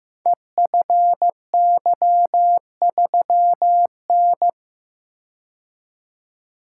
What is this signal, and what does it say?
Morse code, heard as a keyed tone, about 15 wpm, 700 Hz. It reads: EFY3N
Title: Code Groups - 5 Letter:Numbers - 15wpm